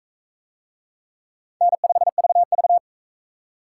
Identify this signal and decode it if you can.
Morse code, heard as a keyed tone, about 41 wpm, 700 Hz. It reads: N5VV